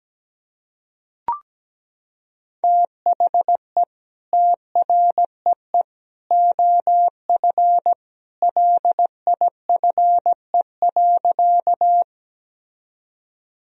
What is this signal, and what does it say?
Morse code, heard as a keyed tone, about 17 wpm, 700 Hz. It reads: THE TREE OF LIFE.
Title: Sentences from Top 1000 Words - Review Part 1 - 17wpm